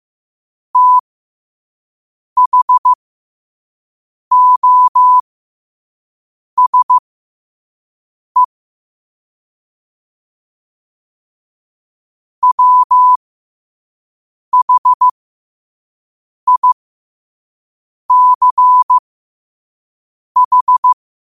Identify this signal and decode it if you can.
Morse code, heard as a keyed tone, about 15 wpm, spaced as for 5 wpm, 1000 Hz. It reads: THOSE WHICH
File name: qotdfiles / 5wpm